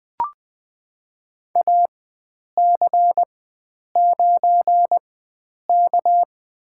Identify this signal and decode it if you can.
Morse code, heard as a keyed tone, about 20 wpm, spaced as for 9 wpm, 700 Hz. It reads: AC9K
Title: USA Call Signs - 4 Characters - 4x Character Spacing - 20wpm